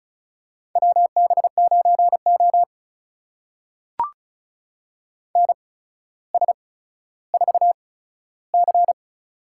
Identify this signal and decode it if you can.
Morse code, heard as a keyed tone, about 36 wpm, spaced as for 15 wpm, 700 Hz. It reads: WB9O NS4C
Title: USA Call Signs - 4 Characters - 8x Character Spacing - 35wpm